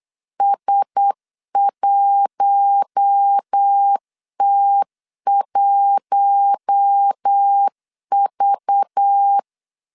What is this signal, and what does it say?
Morse code, heard as a keyed tone, about 8 wpm, 800 Hz. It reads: S1T1V
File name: LEZIONI / 6 PKX0 / 6 PK 8wpm 3003